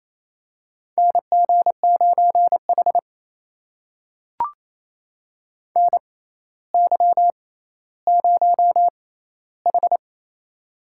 Morse code, heard as a keyed tone, about 27 wpm, spaced as for 14 wpm, 700 Hz. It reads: NG9H NY0H
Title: USA Call Signs - 4 Characters - 6x Character Spacing - 28wpm